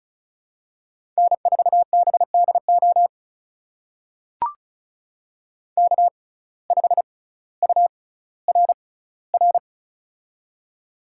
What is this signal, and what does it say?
Morse code, heard as a keyed tone, about 36 wpm, spaced as for 18 wpm, 700 Hz. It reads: N4BDO K5URR
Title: USA Call Signs - 5 Characters - 6x Character Spacing - 35wpm